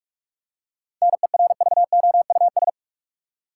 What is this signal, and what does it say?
Morse code, heard as a keyed tone, about 46 wpm, 700 Hz. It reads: NERVOUS
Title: Seven Letter Words - 45wpm